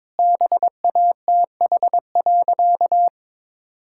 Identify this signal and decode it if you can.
Morse code, heard as a keyed tone, about 21 wpm, 700 Hz. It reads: BATH.
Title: Sentences from Top 1600 Words 22wpm